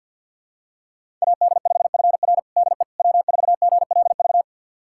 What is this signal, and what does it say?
Morse code, heard as a keyed tone, about 52 wpm, 700 Hz. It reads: AD5FR DE W4GLV
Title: USA Callsign DE Callsign - 5 Characters - 50wpm